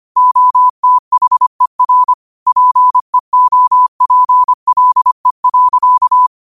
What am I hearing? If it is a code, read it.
Morse code, heard as a keyed tone, about 25 wpm, 1000 Hz. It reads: OTHER PEOPLE.
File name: qotdfiles / 25wpm